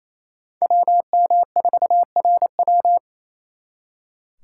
Morse code, heard as a keyed tone, about 35 wpm, 700 Hz. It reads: WM4RW